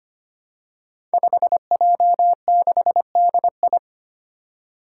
Morse code, heard as a keyed tone, about 25 wpm, 700 Hz. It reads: 5J6DI
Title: Five Challenging Letter-Number Combinations 25wpm